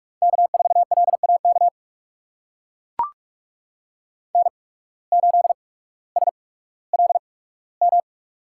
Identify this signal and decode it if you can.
Morse code, heard as a keyed tone, about 44 wpm, spaced as for 19 wpm, 700 Hz. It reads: K4LAK N8SLM
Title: USA Call Signs - 5 Characters - 8x Character Spacing - 45wpm